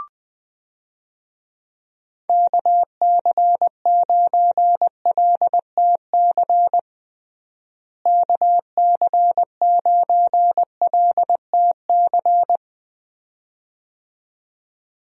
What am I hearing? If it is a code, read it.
Morse code, heard as a keyed tone, about 20 wpm, 700 Hz. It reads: KC9LTC KC9LTC
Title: USA Call Signs - 6 Characters - Repeated 2x - 20wpm